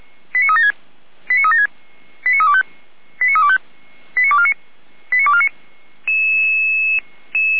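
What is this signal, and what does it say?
Five-tone selective call (ZVEI), heard as quick runs of five tones, tones of about 70 ms. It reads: 89376 89376 89325 89325 89249 89249
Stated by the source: ZVEI five-tone selective call received on an SDR, on the audio server of Feuerwehr Orlamünde.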